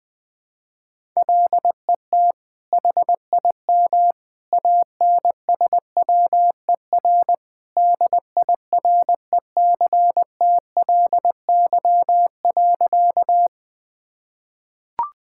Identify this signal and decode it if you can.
Morse code, heard as a keyed tone, about 20 wpm, 700 Hz. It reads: LET HIM ANSWER DIRECTLY.
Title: Sentences from Top 1000 Words - Review Part 5 - 20wpm